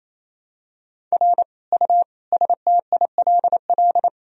Morse code, heard as a keyed tone, about 28 wpm, 700 Hz. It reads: R U STILL
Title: Sentences from Top 400 Words and Abbreviations 28wpm